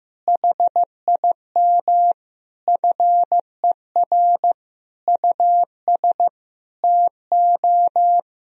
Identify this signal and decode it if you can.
Morse code, heard as a keyed tone, about 15 wpm, 700 Hz. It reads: HIM FER US TO
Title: Sentences from Top 500 Words and Abbreviations 15wpm